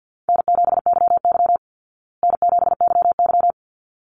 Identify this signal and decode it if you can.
Morse code, heard as a keyed tone, about 50 wpm, 700 Hz. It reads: N7YY N7YY
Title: Easy Call Signs Repeated 2x - 50wpm